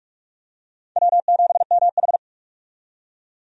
Morse code, heard as a keyed tone, about 46 wpm, 700 Hz. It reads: W7MH
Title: USA Call Signs - 4 Characters - 45wpm